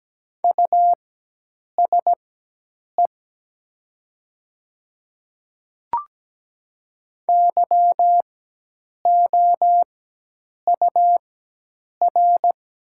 Morse code, heard as a keyed tone, about 17 wpm, spaced as for 8 wpm, 700 Hz. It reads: USE YOUR
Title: 4x Character Spacing - Top 100 Words - 17wpm